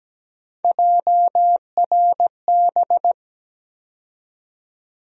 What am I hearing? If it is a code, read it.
Morse code, heard as a keyed tone, about 17 wpm, 700 Hz. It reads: JRB
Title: USA Call Sign Postfixes - 3 Letters 17wpm